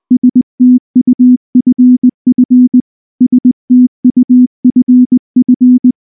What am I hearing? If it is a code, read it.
Morse code, heard as a keyed tone, about 19 wpm, 250 Hz. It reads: STUFF STUFF
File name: morse / morse-training-20WPM-10